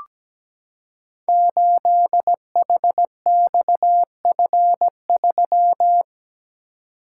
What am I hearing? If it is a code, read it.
Morse code, heard as a keyed tone, about 17 wpm, 700 Hz. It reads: 8HXF3
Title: Code Groups - 5 Letter:Numbers - 17wpm